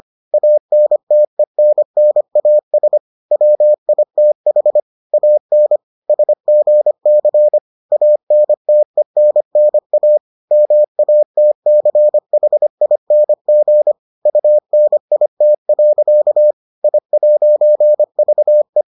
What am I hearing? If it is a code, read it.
Morse code, heard as a keyed tone, about 25 wpm, 600 Hz. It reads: ANTENNAS WITH AN SGC ANTENNA MATCHING UNIT. I'VE